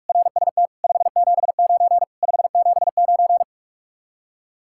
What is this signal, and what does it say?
Morse code, heard as a keyed tone, about 59 wpm, spaced as for 52 wpm, 700 Hz. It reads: RST 579 579
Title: Speed-Racing Common QSO Phrases - 45wpm